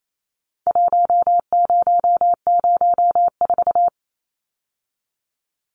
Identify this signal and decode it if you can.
Morse code, heard as a keyed tone, about 28 wpm, 700 Hz. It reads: 1004
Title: Numbers - 4-Digit 28wpm